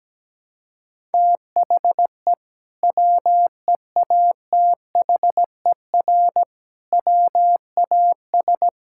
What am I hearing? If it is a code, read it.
Morse code, heard as a keyed tone, about 17 wpm, 700 Hz. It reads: THE WEATHER WAS
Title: Sentences from Top 1000 Words - Review Part 5 - 17wpm